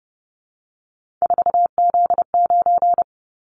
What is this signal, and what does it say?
Morse code, heard as a keyed tone, about 30 wpm, 700 Hz. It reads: 4Z9
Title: Callsign Prefixes for Top 50 DXCC - 30wpm